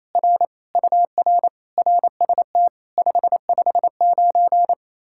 Morse code, heard as a keyed tone, about 37 wpm, 700 Hz. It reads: R UR RST 559